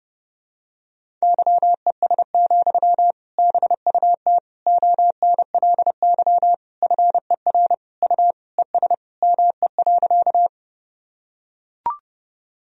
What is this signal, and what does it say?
Morse code, heard as a keyed tone, about 32 wpm, 700 Hz. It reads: YES, BUT ONLY FER U ES ME.
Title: Sentences from Top 500 Words and Abbreviations 30wpm